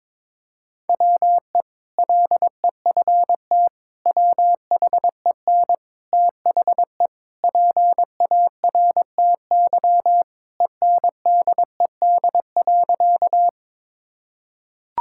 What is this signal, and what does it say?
Morse code, heard as a keyed tone, about 21 wpm, 700 Hz. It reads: WE LEFT WHEN THE PARTY ENDED.
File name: practice / 22wpm / Sentences with 2-Word Phrases - 22wpm